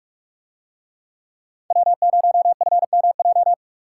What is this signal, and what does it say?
Morse code, heard as a keyed tone, about 44 wpm, 700 Hz. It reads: W0FMJ